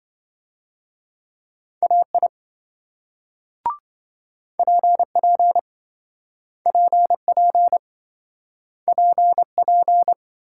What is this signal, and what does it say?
Morse code, heard as a keyed tone, about 27 wpm, 700 Hz. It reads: AI PP PP PP